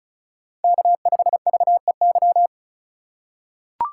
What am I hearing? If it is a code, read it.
Morse code, heard as a keyed tone, about 36 wpm, 700 Hz. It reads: K5VEY
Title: USA Call Signs - 5 Characters - 2x Character Spacing - 35wpm